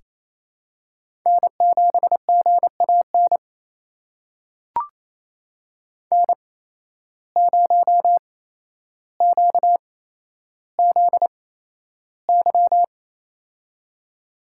Morse code, heard as a keyed tone, about 28 wpm, spaced as for 12 wpm, 700 Hz. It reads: N7GAN N0QZY
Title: USA Call Signs - 5 Characters - 8x Character Spacing - 28wpm